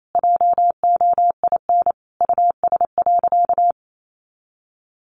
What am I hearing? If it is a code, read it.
Morse code, heard as a keyed tone, about 28 wpm, 700 Hz. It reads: JOIN US.
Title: Sentences from Top 1400 Words 28wpm